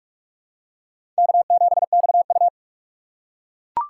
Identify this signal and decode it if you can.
Morse code, heard as a keyed tone, about 44 wpm, 700 Hz. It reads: K7XU
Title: USA Call Signs - 4 Characters - 6x Character Spacing - 45wpm